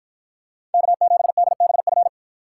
Morse code, heard as a keyed tone, about 53 wpm, 700 Hz. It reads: K7DBF